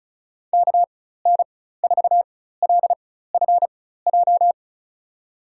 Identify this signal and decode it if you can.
Morse code, heard as a keyed tone, about 34 wpm, spaced as for 16 wpm, 700 Hz. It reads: KN4LFJ
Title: USA Call Signs - 6 Characters - 4x Character Spacing - 35wpm